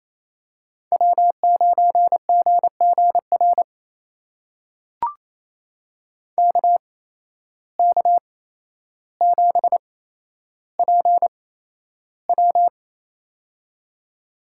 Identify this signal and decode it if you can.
Morse code, heard as a keyed tone, about 27 wpm, spaced as for 12 wpm, 700 Hz. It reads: W9GGR KK7PW
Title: USA Call Signs - 5 Characters - 8x Character Spacing - 28wpm